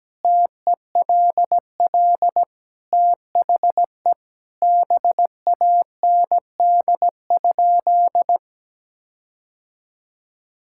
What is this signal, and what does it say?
Morse code, heard as a keyed tone, about 17 wpm, 700 Hz. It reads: TELL THE BAND?
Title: Sentences from Top 1600 Words 17wpm